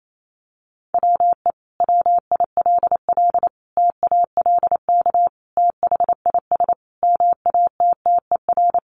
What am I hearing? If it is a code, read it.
Morse code, heard as a keyed tone, about 28 wpm, 700 Hz. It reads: WE WILL TALK THIS MATTER
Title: Sentences from Top 700 Words 28wpm